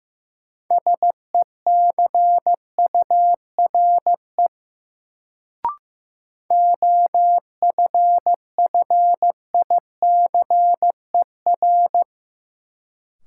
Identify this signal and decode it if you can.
Morse code, heard as a keyed tone, about 15 wpm, 700 Hz. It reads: SECURE OFFICER
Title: Sentences from Top 1200 Words - New Word Drill Alt - 15wpm